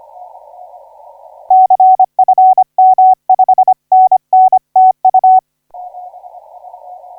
Morse code, heard as a keyed tone, about 25 wpm, 750 Hz. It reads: CFM 5NN TU
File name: VK9NT 0410UTC 12m